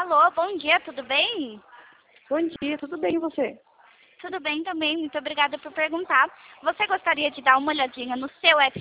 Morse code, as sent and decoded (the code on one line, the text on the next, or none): none